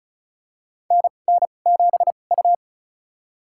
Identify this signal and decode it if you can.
Morse code, heard as a keyed tone, about 36 wpm, spaced as for 25 wpm, 700 Hz. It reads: NN7U
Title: USA Call Signs - 4 Characters - 2x Character Spacing - 35wpm